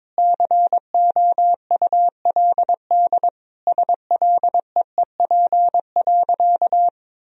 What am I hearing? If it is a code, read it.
Morse code, heard as a keyed tone, about 22 wpm, 700 Hz. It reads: COULD SLEEP.
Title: Sentences from Top 500 Words and Abbreviations 22wpm